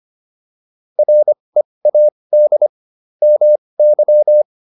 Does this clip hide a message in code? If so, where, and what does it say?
Yes: Morse code from 1.0 to 4.4 s, heard as a keyed tone, about 25 wpm, spaced as for 20 wpm, 600 Hz. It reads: READ MY